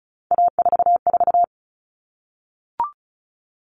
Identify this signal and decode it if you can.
Morse code, heard as a keyed tone, about 34 wpm, 700 Hz. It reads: A44
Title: All Cut Numbers - 3-Digit 35wpm